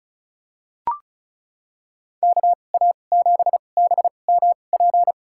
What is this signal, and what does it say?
Morse code, heard as a keyed tone, about 36 wpm, spaced as for 25 wpm, 700 Hz. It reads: KA7BMP
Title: USA Call Signs - 6 Characters - 2x Character Spacing - 35wpm